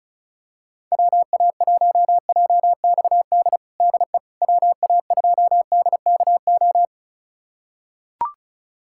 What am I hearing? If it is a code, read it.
Morse code, heard as a keyed tone, about 34 wpm, 690 Hz. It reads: WA1JXD DE WA2DKO
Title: USA Callsign DE Callsign - 6 Characters - 35wpm